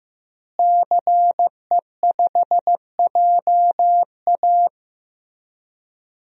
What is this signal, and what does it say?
Morse code, heard as a keyed tone, about 15 wpm, 700 Hz. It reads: CE5JA
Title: DXCC-100 Call Signs - 5 Characters - Repeated 3x - 15wpm